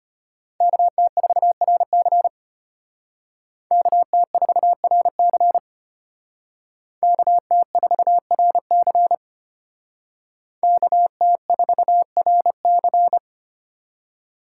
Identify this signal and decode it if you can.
Morse code, heard as a keyed tone, about 32 wpm, 700 Hz. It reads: KT4RC KT4RC KT4RC KT4RC